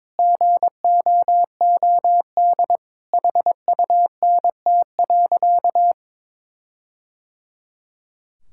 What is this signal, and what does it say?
Morse code, heard as a keyed tone, about 22 wpm, 700 Hz. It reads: GOOD HUNT.
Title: Sentences from Top 1600 Words 22wpm